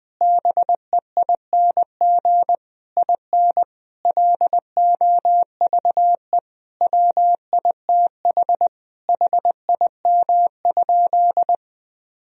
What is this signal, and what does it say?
Morse code, heard as a keyed tone, about 20 wpm, 700 Hz. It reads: BEING IN LOVE WITH HIM?
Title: Sentences from Top 1000 Words - Review Part 3 - 20wpm